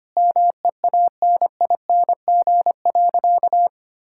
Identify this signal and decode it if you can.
Morse code, heard as a keyed tone, about 25 wpm, 700 Hz. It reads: MEANING.
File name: practice / 25wpm / Sentences with 2-Word Phrases - 25wpm